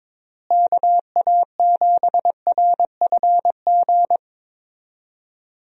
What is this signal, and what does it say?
Morse code, heard as a keyed tone, about 22 wpm, 700 Hz. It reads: KA7RFG
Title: USA Call Signs - 6 Characters - Repeated 2x - 22wpm